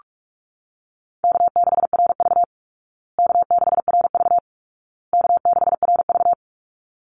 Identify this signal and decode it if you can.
Morse code, heard as a keyed tone, about 44 wpm, 700 Hz. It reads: K6RV K6RV K6RV
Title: Call Signs Repeated 2x - 45wpm